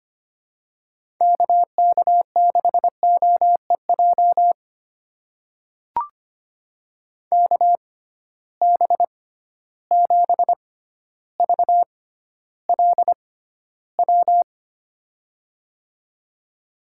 Morse code, heard as a keyed tone, about 25 wpm, spaced as for 13 wpm, 700 Hz. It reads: KK6OEJ KB7VLW
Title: USA Call Signs - 6 Characters - 6x Character Spacing - 25wpm